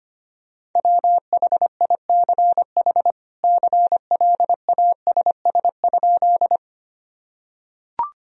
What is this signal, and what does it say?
Morse code, heard as a keyed tone, about 25 wpm, 700 Hz. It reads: WHICH CLASS?
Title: Sentences from Top 1000 Words - Review Part 2 - 25wpm